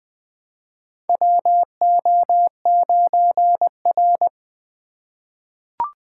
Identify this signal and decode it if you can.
Morse code, heard as a keyed tone, about 20 wpm, 700 Hz. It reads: WO9R
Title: USA Call Signs - 4 Characters - 3x Character Spacing - 20wpm